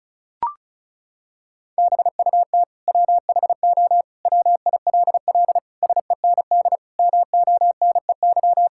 Morse code, heard as a keyed tone, about 36 wpm, 700 Hz. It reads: BUT WHO WILL SEND MONEY
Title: Sentences from Top 1600 Words 35wpm